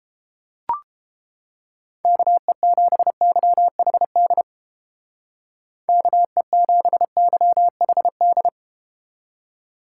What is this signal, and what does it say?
Morse code, heard as a keyed tone, about 32 wpm, 700 Hz. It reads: KE7YHD KE7YHD